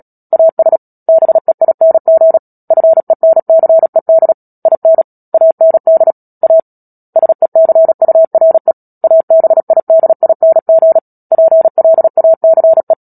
Morse code, heard as a keyed tone, about 37 wpm, 650 Hz. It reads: AS BEING FENCED IN AND A SECURE ABIDING PLACE
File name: Book Two / 10-Chapter 10 - Epilogue - 36 WPM